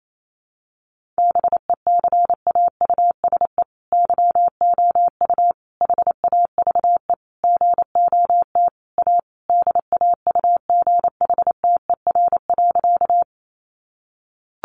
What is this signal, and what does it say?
Morse code, heard as a keyed tone, about 28 wpm, 700 Hz. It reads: BECAUSE YOU HAVE GOT A DAUGHTER.
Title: Sentences from Top 1500 Words 28wpm